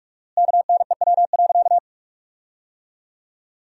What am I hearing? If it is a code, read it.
Morse code, heard as a keyed tone, about 41 wpm, 700 Hz. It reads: KNEW.
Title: 4x Word-Spacing - Sentences from Top 400 Words - 45wpm